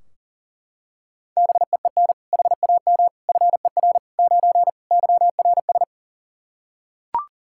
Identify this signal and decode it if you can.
Morse code, heard as a keyed tone, about 40 wpm, 700 Hz. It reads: BEEN HAM FER 9 YRS
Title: Practice QSO - Exchange 3 - More Info - 40wpm